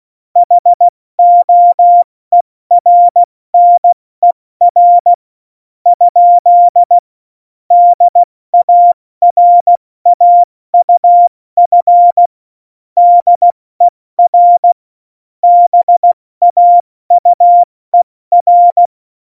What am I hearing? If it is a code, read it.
Morse code, heard as a keyed tone, about 16 wpm, 700 Hz. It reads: HOERNER ? DARAUF DER BAUER
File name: Morsekurs / Witze / Witz 14 Juli 2020, 70 F80